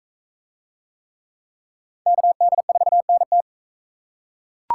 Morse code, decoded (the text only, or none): KD4NT